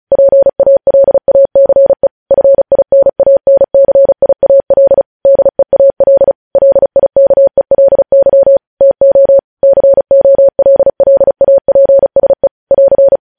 Morse code, heard as a keyed tone, about 36 wpm, 550 Hz. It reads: PALACE FINANCIAL DEAL LIKELY TO COLLAPSE <AR>